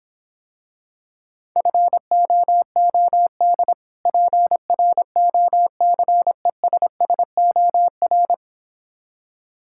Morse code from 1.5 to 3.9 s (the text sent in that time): FOOD